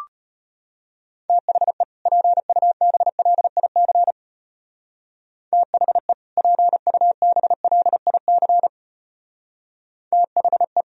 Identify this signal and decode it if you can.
Morse code, heard as a keyed tone, about 30 wpm, 700 Hz. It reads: THE PUBLIC THE PUBLIC THE
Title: Speed-Racing - Sets of 2 Words 25wpm